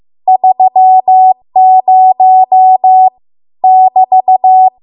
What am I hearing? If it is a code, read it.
Morse code, heard as a keyed tone, about 15 wpm, 750 Hz. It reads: 30 <BT>